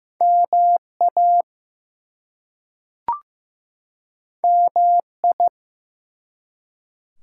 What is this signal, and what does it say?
Morse code, heard as a keyed tone, about 15 wpm, 700 Hz. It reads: MA MI
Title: US States and Canadian Provinces 15wpm